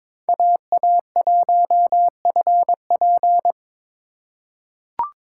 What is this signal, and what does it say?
Morse code, heard as a keyed tone, about 22 wpm, 700 Hz. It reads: AA1FP
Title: USA Call Signs - 5 Characters - 4x Character Spacing - 22wpm